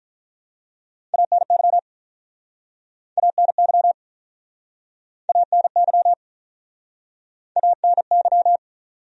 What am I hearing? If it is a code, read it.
Morse code, heard as a keyed tone, about 46 wpm, 700 Hz. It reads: ANY ANY ANY ANY